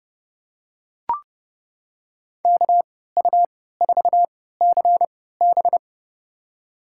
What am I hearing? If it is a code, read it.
Morse code, heard as a keyed tone, about 30 wpm, spaced as for 17 wpm, 700 Hz. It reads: KU4CB
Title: USA Call Signs - 5 Characters - 3x Character Spacing - 30wpm